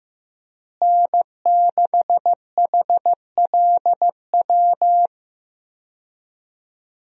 Morse code, heard as a keyed tone, about 15 wpm, 700 Hz. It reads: N6HLW
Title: USA Call Signs - 5 Characters - Repeated 3x - 15wpm